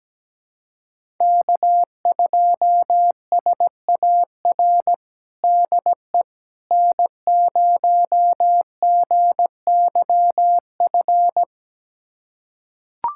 Morse code, heard as a keyed tone, about 17 wpm, 700 Hz. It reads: K2SAR DE N0GYF
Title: USA Callsign DE Callsign - 5 Characters - 17wpm